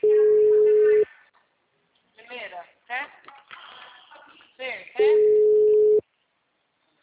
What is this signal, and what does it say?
Ringback tone (Europe, 425 Hz), with speech.